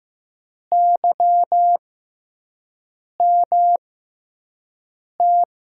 Morse code, heard as a keyed tone, about 16 wpm, spaced as for 5 wpm, 700 Hz. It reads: YMT
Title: USA Call Signs - 5 Characters - 6x Character Spacing - 15wpm